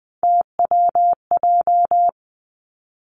Morse code, heard as a keyed tone, about 20 wpm, 700 Hz. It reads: TWJ
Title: Three Challenging Letter-Number Combinations 20wpm